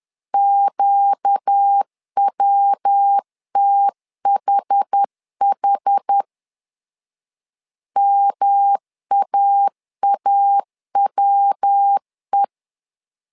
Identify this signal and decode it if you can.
Morse code, heard as a keyed tone, about 10 wpm, 800 Hz. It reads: QWTHH MAAWE
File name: LEZIONI / 4 Q 10wpm 2210